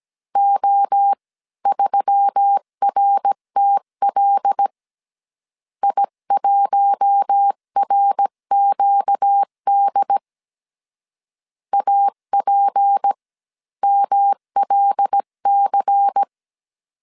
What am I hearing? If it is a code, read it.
Morse code, heard as a keyed tone, about 17 wpm, spaced as for 14 wpm, 800 Hz. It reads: O3RTL I1RQD APMLC
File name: CW / DET / 1 6 1500C S C 16wpm 2733